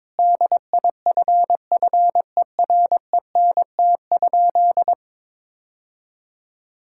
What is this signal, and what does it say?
Morse code, heard as a keyed tone, about 22 wpm, 700 Hz. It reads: DIFFERENT?
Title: Sentences from Top 1600 Words 22wpm